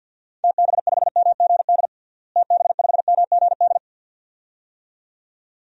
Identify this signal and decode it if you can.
Morse code, heard as a keyed tone, about 53 wpm, 700 Hz. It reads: TB5MGD TB5MGD
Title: DXCC-100 Call Signs - 6 Characters - Repeated 2x - 50wpm